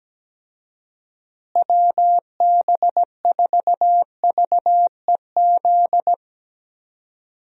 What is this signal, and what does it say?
Morse code, heard as a keyed tone, about 17 wpm, 700 Hz. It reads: WB4VEZ
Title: USA Call Signs - 6 Characters - Repeated 2x - 17wpm